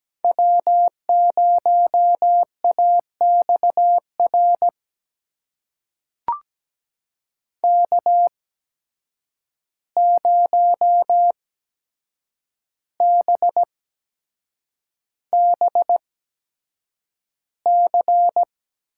Morse code, heard as a keyed tone, about 17 wpm, spaced as for 7 wpm, 700 Hz. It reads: W0AXR K0BBC
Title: USA Call Signs - 5 Characters - 8x Character Spacing - 17wpm